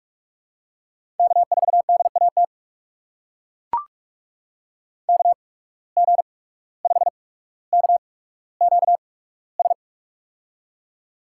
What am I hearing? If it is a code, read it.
Morse code, heard as a keyed tone, about 44 wpm, spaced as for 18 wpm, 700 Hz. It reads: K4DAT KG5KQS
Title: USA Call Signs - 8x Character Spacing - 45wpm